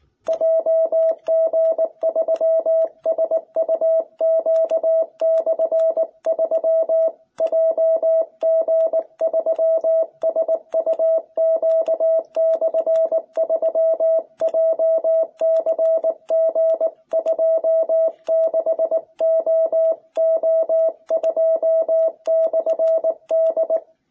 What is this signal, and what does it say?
Morse code, heard as a keyed tone, about 20 wpm, 660 Hz. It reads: JG3SUQ/3 JG3SUQ/3 JCG26OO2/D